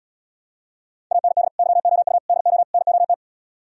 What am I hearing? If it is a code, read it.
Morse code, heard as a keyed tone, about 77 wpm, 690 Hz. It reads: AND YOU DO NOT